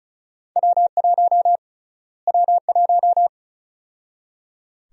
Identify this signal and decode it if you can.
Morse code, heard as a keyed tone, about 34 wpm, 700 Hz. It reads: W1 W1
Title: USA Call Sign Prefixes - Repeated 2x - 35wpm